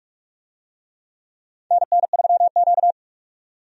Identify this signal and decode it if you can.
Morse code, heard as a keyed tone, about 46 wpm, 700 Hz. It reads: NN3Q
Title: USA Call Signs - 4 Characters - 45wpm